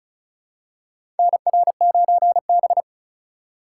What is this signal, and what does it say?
Morse code, heard as a keyed tone, about 34 wpm, 700 Hz. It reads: NR9B
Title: USA Call Signs - 4 Characters - 2x Character Spacing - 35wpm